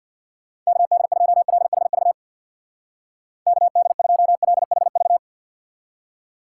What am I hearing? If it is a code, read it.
Morse code, heard as a keyed tone, about 64 wpm, 700 Hz. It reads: KD2LHV KD2LHV